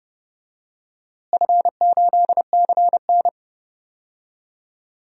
Morse code, heard as a keyed tone, about 30 wpm, 700 Hz. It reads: F8CN
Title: DXCC-100 Call Signs - 4 Characters - Repeated 3x - 30wpm